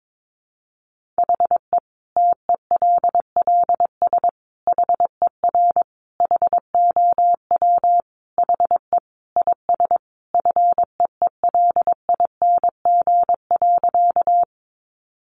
Morse code, heard as a keyed tone, about 22 wpm, 700 Hz. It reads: HE TELLS HER HOW HE IS FEELING.
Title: Additional Easy Phrases 22wpm